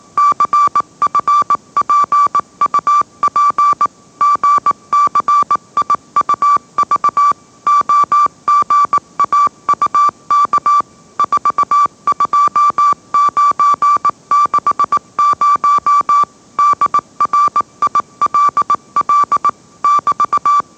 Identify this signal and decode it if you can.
Morse code, heard as a keyed tone, about 18 wpm, 1160 Hz. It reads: CFPUP GCIUV OGAUK 42960 DRILL <BT>
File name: mrx / mrx081 20wpm